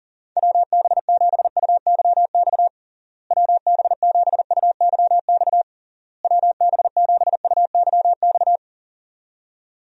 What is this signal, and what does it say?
Morse code, heard as a keyed tone, about 37 wpm, 700 Hz. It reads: WB7UYX WB7UYX WB7UYX
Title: USA Call Signs - 6 Characters - Repeated 3x - 40wpm